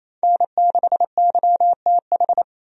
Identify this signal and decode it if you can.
Morse code, heard as a keyed tone, about 27 wpm, 700 Hz. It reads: N6YTH
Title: USA Call Signs - 5 Characters - 28wpm